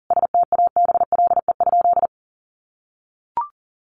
Morse code, heard as a keyed tone, about 40 wpm, 700 Hz. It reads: STABLE?
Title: Sentences from Top 1500 Words 40wpm